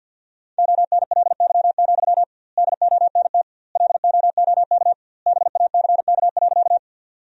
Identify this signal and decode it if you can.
Morse code, heard as a keyed tone, about 50 wpm, 700 Hz. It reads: ONLY, DONT LOOK BACK.
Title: 2x Word-Spacing - Sentences from Top 300 Words - 50wpm